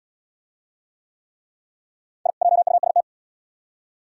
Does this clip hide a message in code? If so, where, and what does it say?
Yes: Morse code from 2.3 to 3.0 s, heard as a keyed tone, about 68 wpm, 700 Hz. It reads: I JUST